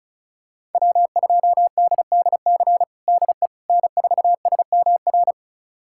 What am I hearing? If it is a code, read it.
Morse code, heard as a keyed tone, about 36 wpm, 700 Hz. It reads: W2DDC DE N4SMR
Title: USA Callsign DE Callsign - 5 Characters - 35wpm